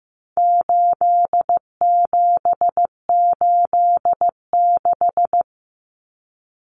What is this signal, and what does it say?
Morse code, heard as a keyed tone, about 15 wpm, 700 Hz. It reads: 8786
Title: Numbers - 4-Digit 15wpm